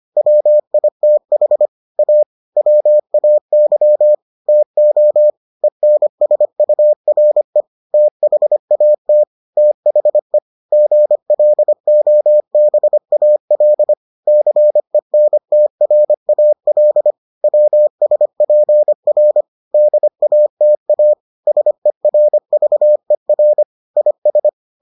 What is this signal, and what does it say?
Morse code, heard as a keyed tone, about 25 wpm, 600 Hz. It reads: WITH A WAY TO ENSURE THAT THE GLOBAL CENTRAL WSPR DATA SERVER IS